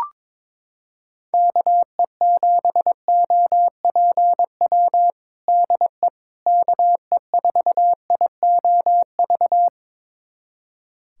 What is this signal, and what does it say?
Morse code, heard as a keyed tone, about 22 wpm, 700 Hz. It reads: KE7OPW DE KE4IOV
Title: USA Callsign DE Callsign - 6 Characters - 22wpm